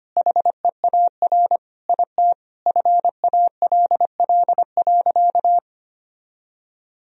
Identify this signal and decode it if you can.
Morse code, heard as a keyed tone, about 25 wpm, 700 Hz. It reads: HEAR IT FALL.
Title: Sentences from Top 1000 Words - Review Part 2 - 25wpm